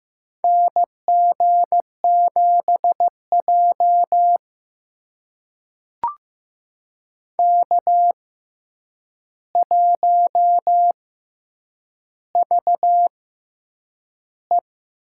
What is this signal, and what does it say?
Morse code, heard as a keyed tone, about 15 wpm, spaced as for 8 wpm, 700 Hz. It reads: NG7J K1VE
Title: USA Call Signs - 4 Characters - 6x Character Spacing - 15wpm